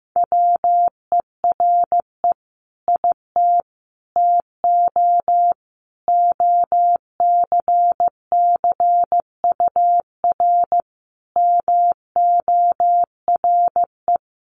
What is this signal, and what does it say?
Morse code, heard as a keyed tone, about 15 wpm, 700 Hz. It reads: WERE IT TO OCCUR MORE
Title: Sentences from Top 1300 Words 15wpm